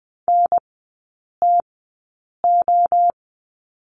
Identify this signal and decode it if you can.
Morse code, heard as a keyed tone, about 20 wpm, spaced as for 8 wpm, 700 Hz. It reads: NTO
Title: 053 Lesson 10e - Letter r - Three Characters 20wpm